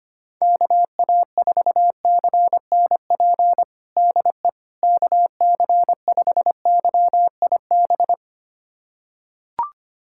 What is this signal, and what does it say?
Morse code, heard as a keyed tone, about 25 wpm, 700 Hz. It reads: KA4CNP DE KC5YIB
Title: USA Callsign DE Callsign - 6 Characters - 25wpm